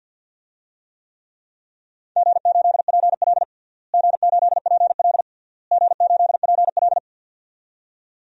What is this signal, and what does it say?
Morse code, heard as a keyed tone, about 50 wpm, 700 Hz. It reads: G8PL G8PL G8PL
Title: DXCC-100 Call Signs - 4 Characters - Repeated 3x - 50wpm